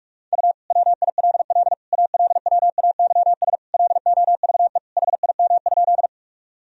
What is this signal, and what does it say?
Morse code, heard as a keyed tone, about 41 wpm, 700 Hz. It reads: U WILL ALWAYS LOVE HIM?
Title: Sentences from Top 400 Words and Abbreviations 45wpm